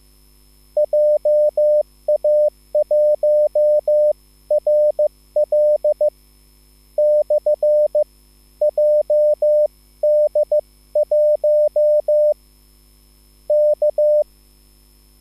Morse code, heard as a keyed tone, about 15 wpm, spaced as for 11 wpm, 600 Hz. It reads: JA1RL/JD1 K
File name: QSL / CW lesson